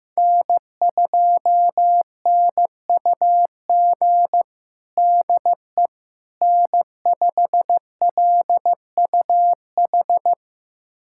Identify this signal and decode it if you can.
Morse code, heard as a keyed tone, about 15 wpm, 700 Hz. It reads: N2NUG DE N5LUH
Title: USA Callsign DE Callsign - 5 Characters - 15wpm